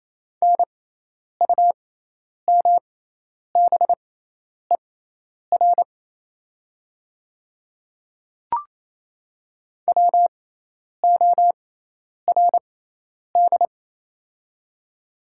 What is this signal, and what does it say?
Morse code, heard as a keyed tone, about 28 wpm, spaced as for 10 wpm, 700 Hz. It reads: NUMBER WORD
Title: 6x Character Spacing - Top 200 Words - 28wpm